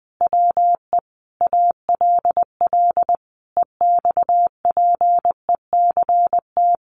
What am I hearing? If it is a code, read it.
Morse code, heard as a keyed tone, about 20 wpm, 700 Hz. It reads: WE ALL EXPECT
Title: Sentences from Top 1100 Words 20wpm